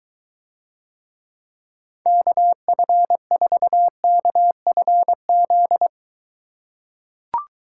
Morse code, heard as a keyed tone, about 23 wpm, 700 Hz. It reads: KF4KFZ